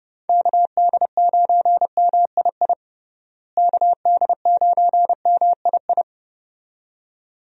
Morse code, heard as a keyed tone, about 32 wpm, 700 Hz. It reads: KD9MII KD9MII
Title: USA Call Signs - 6 Characters - Repeated 2x - 30wpm